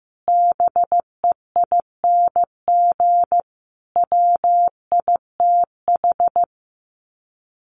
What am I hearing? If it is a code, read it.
Morse code, heard as a keyed tone, about 15 wpm, 700 Hz. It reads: BEING WITH
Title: Sets of 5 Words 15wpm